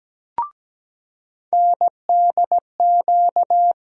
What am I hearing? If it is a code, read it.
Morse code, heard as a keyed tone, about 17 wpm, 700 Hz. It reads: NDQ
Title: USA Call Sign Postfixes - 3 Letters 17wpm